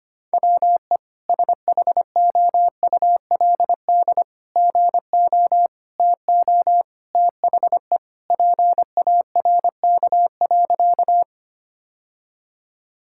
Morse code, heard as a keyed tone, about 25 wpm, 700 Hz. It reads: WE SHOULD GO TO THE PARK.
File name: practice / 25wpm / Sentences with 3-Word Phrases - 25wpm